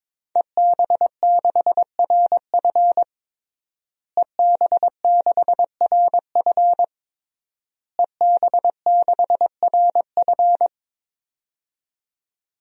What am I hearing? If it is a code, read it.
Morse code, heard as a keyed tone, about 22 wpm, 700 Hz. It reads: EB6RF EB6RF EB6RF
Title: DXCC-100 Call Signs - 5 Characters - Repeated 3x - 22wpm